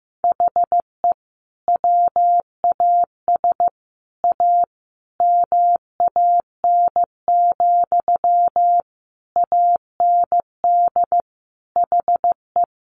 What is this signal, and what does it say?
Morse code, heard as a keyed tone, about 15 wpm, 700 Hz. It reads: HE WAS A MAN, AND HE
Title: Sentences from Top 1500 Words 15wpm